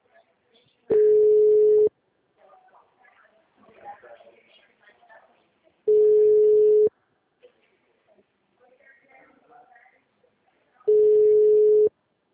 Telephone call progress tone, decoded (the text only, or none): ringback tone